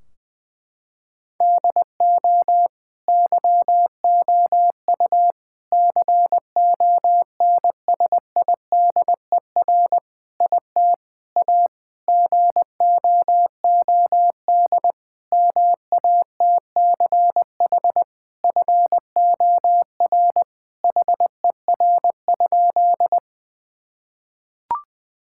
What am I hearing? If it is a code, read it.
Morse code, heard as a keyed tone, about 20 wpm, 700 Hz. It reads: DO YOU CONSIDER IT A GOOD MATCH FOR HER?
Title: Sentences from Top 1600 Words 20wpm